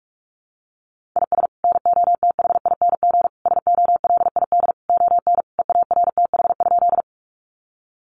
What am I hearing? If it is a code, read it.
Morse code, heard as a keyed tone, about 44 wpm, 700 Hz. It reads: IS NOTHING SOLID ON EARTH?
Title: Sentences from Top 1500 Words 45wpm